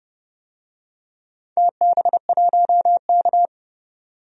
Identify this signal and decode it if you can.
Morse code, heard as a keyed tone, about 29 wpm, 700 Hz. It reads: TB1K
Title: DXCC-100 Call Signs - 4 Characters - 30wpm